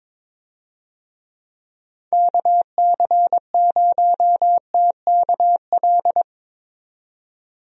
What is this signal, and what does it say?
Morse code, heard as a keyed tone, about 22 wpm, 700 Hz. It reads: KC0TKL
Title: USA Call Signs - 6 Characters - 22wpm